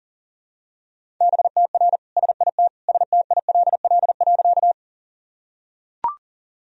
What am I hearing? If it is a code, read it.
Morse code, heard as a keyed tone, about 40 wpm, 700 Hz. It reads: BTR SIT STILL.
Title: Sentences from Top 500 Words and Abbreviations 40wpm